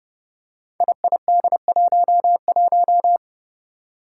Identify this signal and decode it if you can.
Morse code, heard as a keyed tone, about 30 wpm, 700 Hz. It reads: IID11
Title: Five Challenging Letter-Number Combinations 30wpm